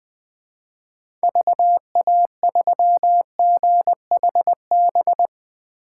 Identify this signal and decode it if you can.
Morse code, heard as a keyed tone, about 20 wpm, 700 Hz. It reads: VA3GHB
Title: 2023 POTA Activator Call Signs - 20wpm